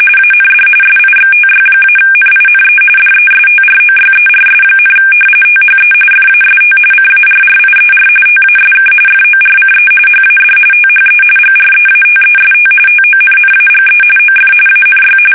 RTTY, about 45.45 baud. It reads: AND SIXTY-NINE DESTROYERS, THE FAST CARRIER TASK FORCE HAD THE GREATEST TONNAGE OF ANY
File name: fsk850 / battleshipnj